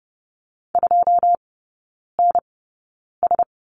Morse code, heard as a keyed tone, about 30 wpm, spaced as for 9 wpm, 700 Hz. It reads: 2NS
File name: Lesson 19e - Letter w - Three Characters 30wpm